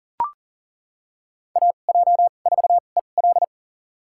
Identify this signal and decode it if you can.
Morse code, heard as a keyed tone, about 37 wpm, spaced as for 28 wpm, 700 Hz. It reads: AJ4EL
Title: USA Call Signs - 5 Characters - 2x Character Spacing - 40wpm